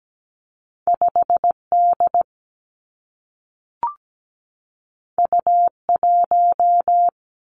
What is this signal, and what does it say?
Morse code, heard as a keyed tone, about 17 wpm, 700 Hz. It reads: 5D U1
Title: Two Challenging Letter-Number Combinations 17wpm